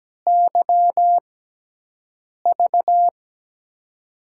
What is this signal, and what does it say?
Morse code, heard as a keyed tone, about 17 wpm, spaced as for 6 wpm, 700 Hz. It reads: YV
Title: USA Call Signs - 5 Characters - 6x Character Spacing - 17wpm